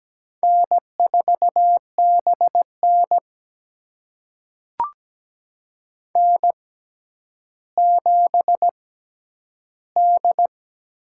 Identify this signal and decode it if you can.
Morse code, heard as a keyed tone, about 17 wpm, spaced as for 10 wpm, 700 Hz. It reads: N4BN N7D
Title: USA Call Signs - 6x Character Spacing - 17wpm